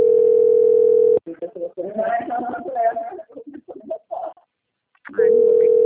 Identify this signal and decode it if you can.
Ringback tone (North America), 440 + 480 Hz, with speech.
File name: out- - - - - .14554